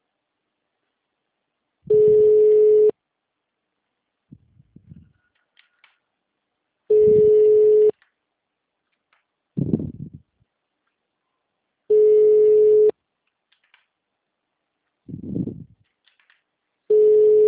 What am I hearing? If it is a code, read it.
Ringback tone (Europe, 425 Hz).
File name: out- -2011- - - .0